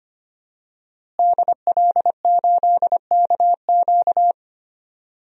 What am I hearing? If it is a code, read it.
Morse code, heard as a keyed tone, about 25 wpm, 700 Hz. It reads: DL8KQ
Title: DXCC-100 Call Signs - 5 Characters - Repeated 3x - 25wpm